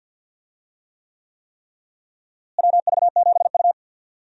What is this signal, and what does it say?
Morse code, heard as a keyed tone, about 52 wpm, 700 Hz. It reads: WV7U
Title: USA Call Signs - 4 Characters - Repeated 3x - 50wpm